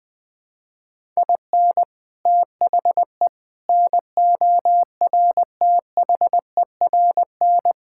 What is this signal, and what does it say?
Morse code, heard as a keyed tone, about 21 wpm, 700 Hz. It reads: IN THE NORTHERN
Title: Sets of 6 Words - 20wpm